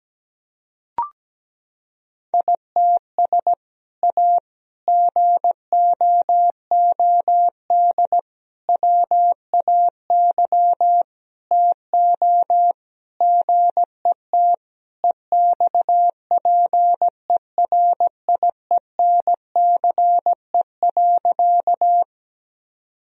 Morse code, heard as a keyed tone, about 17 wpm, 700 Hz. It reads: ITS A GOOD WAY TO GET EXPERIENCE.
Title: Sentences from Top 1000 Words - Review Part 3 - 17wpm